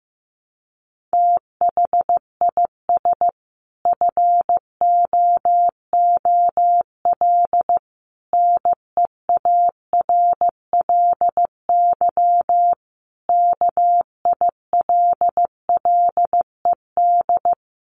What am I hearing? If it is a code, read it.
Morse code, heard as a keyed tone, about 15 wpm, 700 Hz. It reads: THIS FOOL NEARLY KILLED
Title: A Sound Of Thunder 15wpm